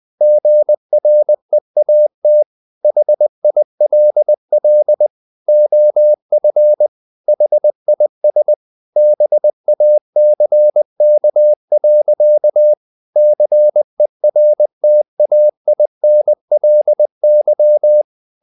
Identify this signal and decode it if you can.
Morse code, heard as a keyed tone, about 20 wpm, 600 Hz. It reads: GREAT HILL OF HIS BACK. CERTAINLY
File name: the old man and the sea / the old man and the sea part 1 20wpm